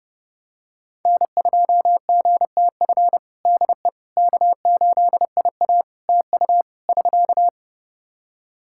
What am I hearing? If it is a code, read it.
Morse code, heard as a keyed tone, about 30 wpm, 700 Hz. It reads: N2GTF DE K8IA TU <SK>
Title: Practice QSO - Exchange 4 - Ending - 30wpm